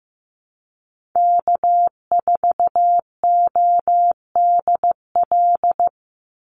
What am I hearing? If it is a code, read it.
Morse code, heard as a keyed tone, about 15 wpm, 700 Hz. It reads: K4ODL
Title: Call Signs Repeated 3x - 15wpm